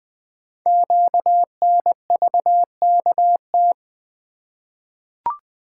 Morse code, heard as a keyed tone, about 20 wpm, 700 Hz. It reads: QNVKT
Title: Code Groups - 5 Letter:Numbers - 20wpm